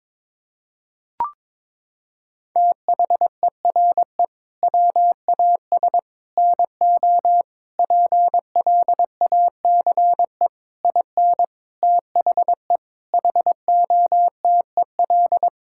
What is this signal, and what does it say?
Morse code, heard as a keyed tone, about 22 wpm, 700 Hz. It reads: THERE WAS NO PLACE IN THE HOTEL
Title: Sentences from Top 1600 Words 22wpm